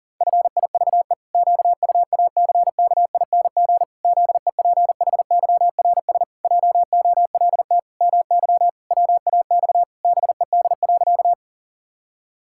Morse code, heard as a keyed tone, about 37 wpm, 700 Hz. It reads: FIVE QUACKING ZEPHYRS JOLT MY WAX BED.
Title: Pangram Challenge v2 - 40wpm